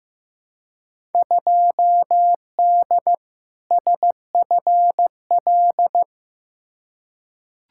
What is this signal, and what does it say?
Morse code, heard as a keyed tone, about 15 wpm, 700 Hz. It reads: 2D SFL
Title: Field Day Exchanges Encore - 15wpm